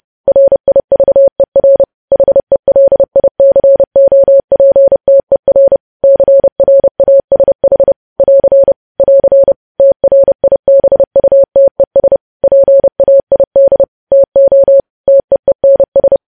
Morse code, heard as a keyed tone, about 30 wpm, 550 Hz. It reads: RIVER HELICOPTER CRASH <AR> <AR> TRIBUTES PAID TO TEENS
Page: news headlines in morse code at 30 wpm